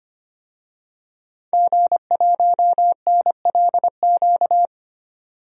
Morse code, heard as a keyed tone, about 25 wpm, 700 Hz. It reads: G1NLQ
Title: DXCC-100 Call Signs - 5 Characters - 25wpm